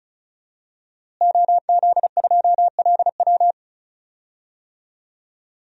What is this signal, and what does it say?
Morse code, heard as a keyed tone, about 34 wpm, 700 Hz. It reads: OZ2LW
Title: DXCC-100 Call Signs - 5 Characters - Repeated 3x - 35wpm